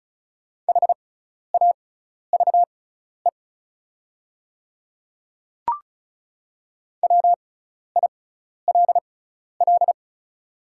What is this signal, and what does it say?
Morse code, heard as a keyed tone, about 34 wpm, spaced as for 12 wpm, 690 Hz. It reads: HAVE WILL
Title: 6x Character Spacing - Top 200 Words - 35wpm